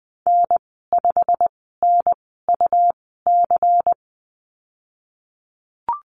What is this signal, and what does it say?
Morse code, heard as a keyed tone, about 20 wpm, spaced as for 15 wpm, 700 Hz. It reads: N5NUC
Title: 122 Lesson 20g - Number 3 - Call Signs 20wpm